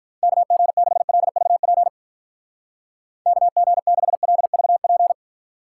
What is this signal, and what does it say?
Morse code, heard as a keyed tone, about 44 wpm, 700 Hz. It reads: KG6LVP KG6LVP